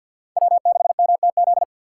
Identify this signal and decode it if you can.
Morse code, heard as a keyed tone, about 66 wpm, spaced as for 58 wpm, 700 Hz. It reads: W6MTZ
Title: Speed Racing - USA 1x3 Call Signs 50wpm